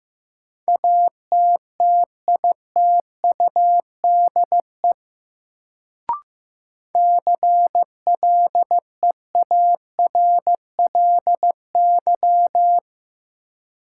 Morse code, heard as a keyed tone, about 15 wpm, 700 Hz. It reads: ATTITUDE CLEARLY
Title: Sentences from Top 1300 Words - New Word Drill - 15wpm